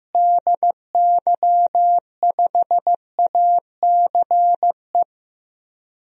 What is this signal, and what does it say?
Morse code, heard as a keyed tone, about 15 wpm, 700 Hz. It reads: DY5ACE
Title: DXCC-100 Call Signs - 6 Characters - Repeated 2x - 15wpm